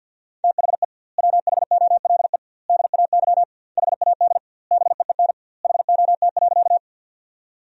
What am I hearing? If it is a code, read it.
Morse code, heard as a keyed tone, about 51 wpm, 700 Hz. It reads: THE WHOLE DAY HAD BEEN HOT.
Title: 2x Word-Spacing - Sentences from Top 500 Words - 50wpm